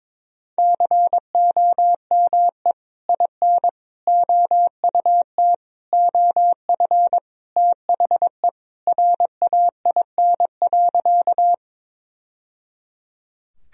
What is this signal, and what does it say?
Morse code, heard as a keyed tone, about 22 wpm, 700 Hz. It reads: COME IN OUT OF THE RAIN.
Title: Sentences from Top 1000 Words - Review Part 1 - 22wpm